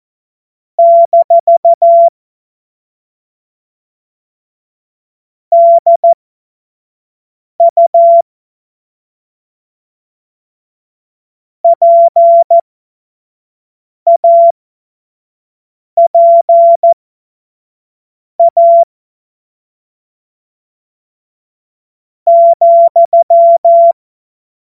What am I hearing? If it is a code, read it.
Morse code, heard as a keyed tone, about 14 wpm, spaced as for 5 wpm, 680 Hz. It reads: - DU PAPA ,
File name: Morsekurs / Witze / Witz 15. Juni 2021, 25 F70